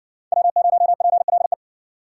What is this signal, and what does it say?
Morse code, heard as a keyed tone, about 60 wpm, 700 Hz. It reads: W0PLE